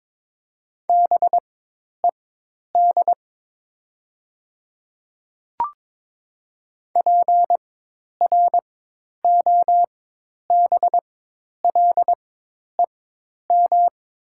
Morse code, heard as a keyed tone, about 22 wpm, spaced as for 10 wpm, 700 Hz. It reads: BED PROBLEM
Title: 4x Character Spacing - Top 500 Words - 22wpm